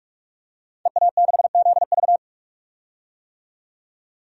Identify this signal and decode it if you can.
Morse code, heard as a keyed tone, about 42 wpm, 700 Hz. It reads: EA6ZV
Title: DXCC-100 Call Signs - 5 Characters - 45wpm